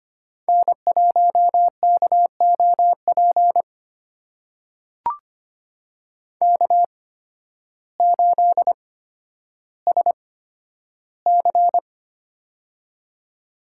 Morse code, heard as a keyed tone, about 25 wpm, spaced as for 12 wpm, 700 Hz. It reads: N1KOP K8SC
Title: USA Call Signs - 8x Character Spacing - 25wpm